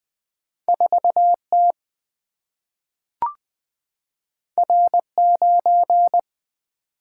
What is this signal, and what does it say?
Morse code, heard as a keyed tone, about 20 wpm, 700 Hz. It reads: 4T R9
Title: Single-Letter Single-Number 20wpm